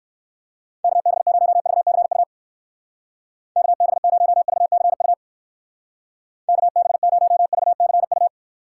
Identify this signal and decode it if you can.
Morse code, heard as a keyed tone, about 59 wpm, 700 Hz. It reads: KB0VCU KB0VCU KB0VCU